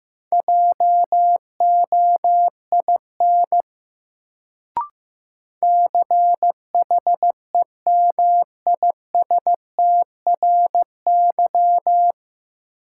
Morse code, heard as a keyed tone, about 15 wpm, 700 Hz. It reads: JOIN CHEMISTRY
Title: Sentences from Top 1000 Words - Review Part 6 - New Word Drill - 15wpm